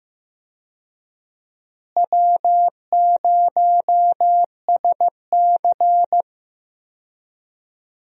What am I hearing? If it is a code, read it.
Morse code, heard as a keyed tone, about 15 wpm, 700 Hz. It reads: W0SC
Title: USA Call Signs - 4 Characters - Repeated 3x - 15wpm